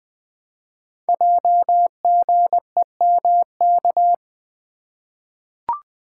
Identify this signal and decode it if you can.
Morse code, heard as a keyed tone, about 20 wpm, 700 Hz. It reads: JGEMK
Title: Code Groups - 5 Letters - 20wpm